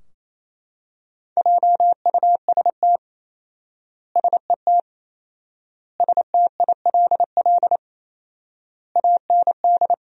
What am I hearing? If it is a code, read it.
Morse code, heard as a keyed tone, about 28 wpm, 700 Hz. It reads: JUST SET STILL AND
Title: 4x Word-Spacing - Sentences from Top 300 Words - 28wpm